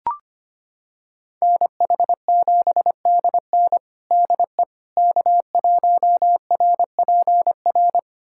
Morse code, heard as a keyed tone, about 25 wpm, 700 Hz. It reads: NH7DN DE K1RPR
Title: USA Callsign DE Callsign 25wpm